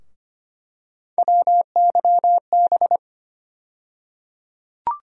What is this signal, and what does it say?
Morse code, heard as a keyed tone, about 25 wpm, 700 Hz. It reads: WYB
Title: USA Call Sign Postfixes - 3 Letters 25wpm